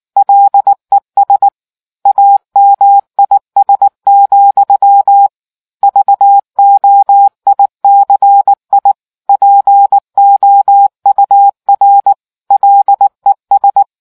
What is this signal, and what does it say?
Morse code, heard as a keyed tone, about 19 wpm, 800 Hz. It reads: LES AMIS, VOICI POUR LES